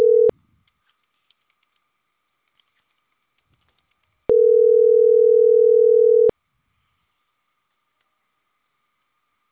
Ringback tone (North America), 440 + 480 Hz.